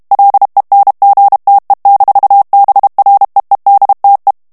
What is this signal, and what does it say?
Morse code, heard as a keyed tone, about 32 wpm, 800 Hz. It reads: LENGTE-BREEDTE
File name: morse mp3 / HOLLANDAIS texte clair 1 vitesse 30 mots minute